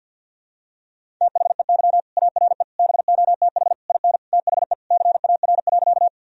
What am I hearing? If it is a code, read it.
Morse code, heard as a keyed tone, about 52 wpm, 700 Hz. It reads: THEY ARE BOTH IN THE CAR.
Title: Sentences from Top 1000 Words - Review Part 1 - 50wpm